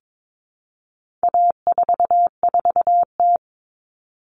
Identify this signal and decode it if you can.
Morse code, heard as a keyed tone, about 22 wpm, 700 Hz. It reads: A44T
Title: All Cut Numbers - 4-Digit 22wpm